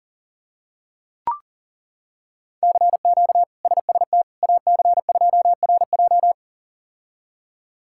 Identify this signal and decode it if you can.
Morse code, heard as a keyed tone, about 40 wpm, 700 Hz. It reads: CQ SST AC2RJ
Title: K1USN SST QSO Practice - 40wpm